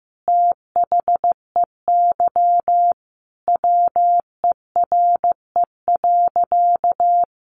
Morse code, heard as a keyed tone, about 15 wpm, 700 Hz. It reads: THEY WERE.
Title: Sentences from Top 900 Words 15wpm